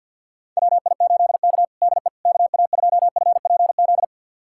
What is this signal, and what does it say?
Morse code, heard as a keyed tone, about 51 wpm, 700 Hz. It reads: WI8K DE KA2FPZ
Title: USA Callsign DE Callsign 50wpm